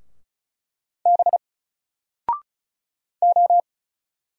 Morse code, heard as a keyed tone, about 36 wpm, 700 Hz. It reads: B O
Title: Single Letter-Number 35wpm